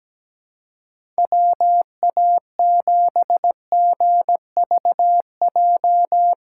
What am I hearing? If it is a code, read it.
Morse code, heard as a keyed tone, about 17 wpm, 700 Hz. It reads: WA7GVJ
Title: USA Call Signs - 6x Character Spacing - 17wpm